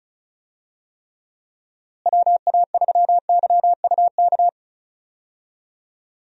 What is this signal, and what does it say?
Morse code, heard as a keyed tone, about 34 wpm, 700 Hz. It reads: WA3YUK
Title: USA Call Signs - 6 Characters - Repeated 2x - 35wpm